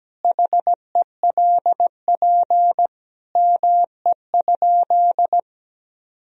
Morse code, heard as a keyed tone, about 17 wpm, 700 Hz. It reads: HELP ME?
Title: Sentences from Top 1000 Words - Review Part 3 - 17wpm